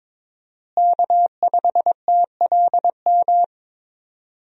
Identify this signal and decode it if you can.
Morse code, heard as a keyed tone, about 22 wpm, 700 Hz. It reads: K5TLM
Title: USA Call Signs - 5 Characters - Repeated 3x - 22wpm